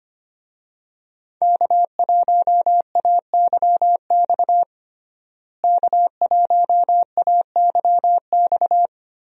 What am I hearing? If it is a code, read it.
Morse code, heard as a keyed tone, about 25 wpm, 700 Hz. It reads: K1AYX K1AYX
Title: USA Call Signs - 5 Characters - Repeated 3x - 25wpm